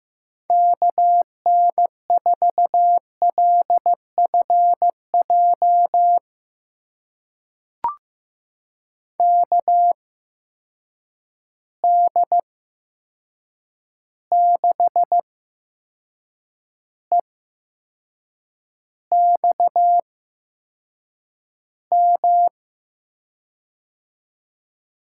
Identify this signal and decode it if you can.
Morse code, heard as a keyed tone, about 15 wpm, spaced as for 6 wpm, 700 Hz. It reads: KN4LFJ KD6EXM
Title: USA Call Signs - 6 Characters - 8x Character Spacing - 15wpm